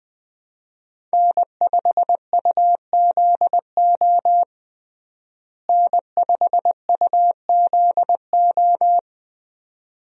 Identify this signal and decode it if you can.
Morse code, heard as a keyed tone, about 20 wpm, 700 Hz. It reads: N5UZO N5UZO
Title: USA Call Signs - 5 Characters - Repeated 2x - 20wpm